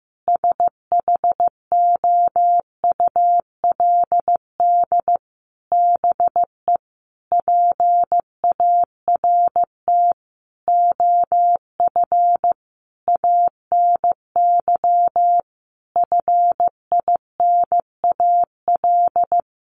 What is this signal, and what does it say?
Morse code, heard as a keyed tone, about 15 wpm, 700 Hz. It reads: SHOULD BE PART OF ANY FINAL
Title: Sentences from Top 1500 Words 15wpm